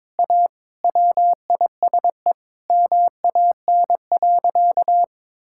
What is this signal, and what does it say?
Morse code, heard as a keyed tone, about 21 wpm, 700 Hz. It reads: A WISE MAN.
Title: Sentences from Top 1600 Words 22wpm